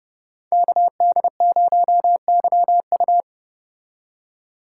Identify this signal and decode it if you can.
Morse code, heard as a keyed tone, about 29 wpm, 700 Hz. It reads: KD0YU